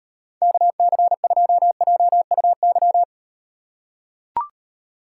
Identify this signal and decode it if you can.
Morse code, heard as a keyed tone, about 38 wpm, 700 Hz. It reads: KC2JUY